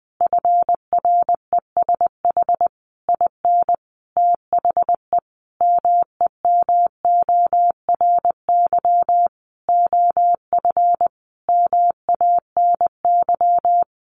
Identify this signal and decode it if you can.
Morse code, heard as a keyed tone, about 20 wpm, 700 Hz. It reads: FRESH IN THE MEMORY OF MANY
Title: Sentences from Top 1300 Words 20wpm